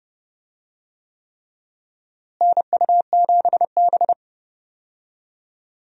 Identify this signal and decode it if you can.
Morse code, heard as a keyed tone, about 30 wpm, 700 Hz. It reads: NU7B
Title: USA Call Signs - 4 Characters - Repeated 3x - 30wpm